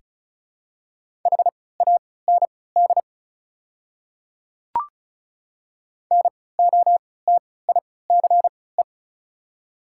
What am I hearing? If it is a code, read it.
Morse code, heard as a keyed tone, about 36 wpm, spaced as for 20 wpm, 700 Hz. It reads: HAND NOTICE
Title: 3x Character Spacing - Top 500 Words - 35wpm